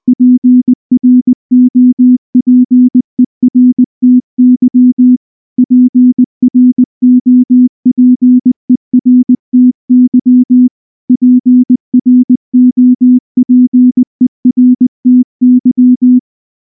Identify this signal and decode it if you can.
Morse code, heard as a keyed tone, about 19 wpm, 250 Hz. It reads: PROPERTY PROPERTY PROPERTY